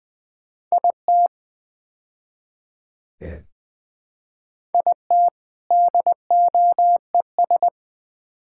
Morse code, heard as a keyed tone, about 20 wpm, 700 Hz. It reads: IT IT DOES
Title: Build A Phrase - 5-Word Phrases - 20wpm